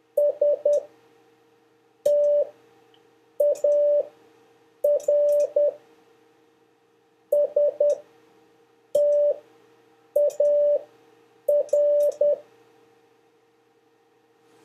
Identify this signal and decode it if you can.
Morse code, heard as a keyed tone, about 10 wpm, spaced as for 6 wpm, 580 Hz. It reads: STAR STAR
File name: 4of5-words-3